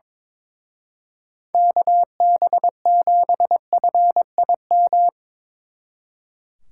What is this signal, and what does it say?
Morse code, heard as a keyed tone, about 23 wpm, 700 Hz. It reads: KB7FIM
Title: Speed Racing - USA 2x3 Call Signs 22wpm